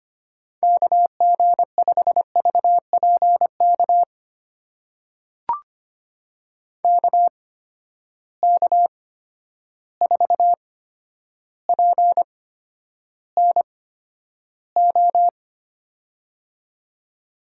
Morse code, heard as a keyed tone, about 25 wpm, spaced as for 11 wpm, 700 Hz. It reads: KG5VPK KK4PNO
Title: USA Call Signs - 6 Characters - 8x Character Spacing - 25wpm